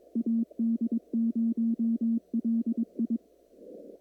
Morse code, heard as a keyed tone, about 21 wpm, 230 Hz. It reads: AD0LI
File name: CW / kt5j wpx cw-op-k5pi